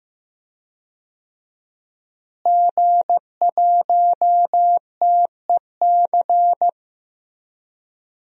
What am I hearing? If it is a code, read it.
Morse code, heard as a keyed tone, about 14 wpm, 700 Hz. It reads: G1TEC
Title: DXCC-100 Call Signs - 5 Characters - Repeated 3x - 15wpm